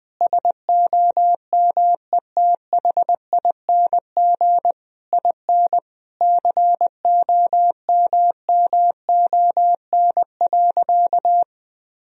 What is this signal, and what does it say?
Morse code, heard as a keyed tone, about 20 wpm, 700 Hz. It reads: SOMETHING IN COMMON.
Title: Sentences from Top 1000 Words - Review Part 3 - 20wpm